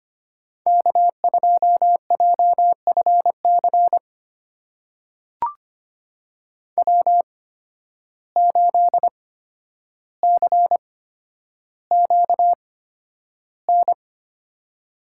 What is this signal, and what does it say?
Morse code, heard as a keyed tone, about 25 wpm, spaced as for 11 wpm, 700 Hz. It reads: K2JFC W8CQN
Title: USA Call Signs - 5 Characters - 8x Character Spacing - 25wpm